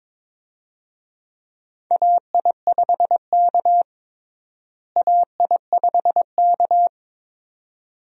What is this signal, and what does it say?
Morse code, heard as a keyed tone, about 22 wpm, 700 Hz. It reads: AI5K AI5K
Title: USA Call Signs - 4 Characters - Repeated 2x - 22wpm